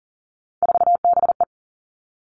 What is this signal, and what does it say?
Morse code, heard as a keyed tone, about 42 wpm, 700 Hz. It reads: 4BE
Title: Three Challenging Letter-Number Combinations 40wpm